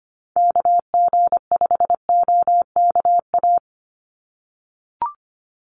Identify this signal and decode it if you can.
Morse code, heard as a keyed tone, about 25 wpm, 700 Hz. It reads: KG5OKA
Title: USA Call Signs 25wpm